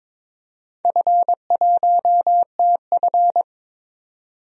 Morse code, heard as a keyed tone, about 22 wpm, 700 Hz. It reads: F1TF
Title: DXCC-100 Call Signs - 4 Characters - 22wpm